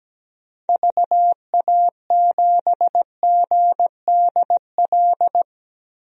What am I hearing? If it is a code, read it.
Morse code, heard as a keyed tone, about 17 wpm, 700 Hz. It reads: VA7GDL
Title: DXCC-100 Call Signs - 6 Characters - 17wpm